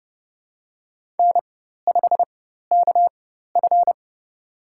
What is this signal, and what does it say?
Morse code, heard as a keyed tone, about 32 wpm, spaced as for 14 wpm, 700 Hz. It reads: N5KF
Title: USA Call Signs - 4x Character Spacing - 30wpm